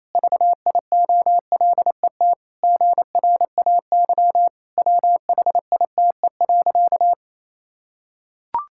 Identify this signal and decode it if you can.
Morse code, heard as a keyed tone, about 28 wpm, 700 Hz. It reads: VIOLET GRAY WHITE.
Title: Pangram Challenge v2 - 28wpm